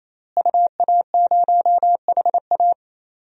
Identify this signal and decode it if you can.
Morse code, heard as a keyed tone, about 28 wpm, 700 Hz. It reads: UA0HA
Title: DXCC-100 Call Signs - 5 Characters - Repeated 3x - 28wpm